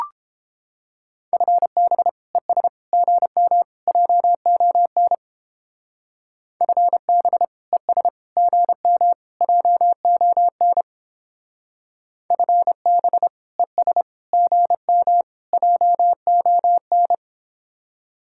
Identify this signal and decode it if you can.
Morse code, heard as a keyed tone, about 29 wpm, 700 Hz. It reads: FB ES GM JON FB ES GM JON FB ES GM JON